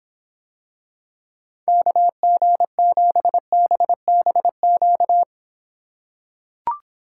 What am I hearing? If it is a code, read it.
Morse code, heard as a keyed tone, about 26 wpm, 700 Hz. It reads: KG7BBQ